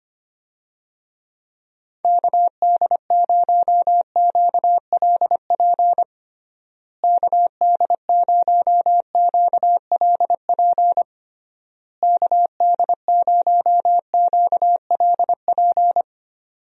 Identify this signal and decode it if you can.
Morse code, heard as a keyed tone, about 25 wpm, 700 Hz. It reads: KD0QLP KD0QLP KD0QLP
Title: USA Call Signs - 6 Characters - Repeated 3x - 25wpm